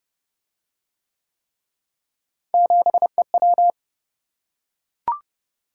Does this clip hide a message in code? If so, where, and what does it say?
Yes: Morse code from 2.5 to 3.7 s, heard as a keyed tone, about 29 wpm, 700 Hz. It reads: 7EW